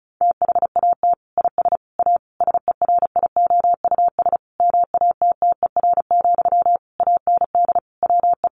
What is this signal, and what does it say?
Morse code, heard as a keyed tone, about 34 wpm, 700 Hz. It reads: THAT IS A SERIOUS MATTER, AND WE
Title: Sentences from Top 1100 Words 35wpm